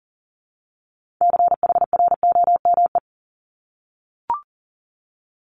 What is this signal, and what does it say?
Morse code, heard as a keyed tone, about 38 wpm, 690 Hz. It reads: CHROME
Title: Multi-letter Phonemes 40wpm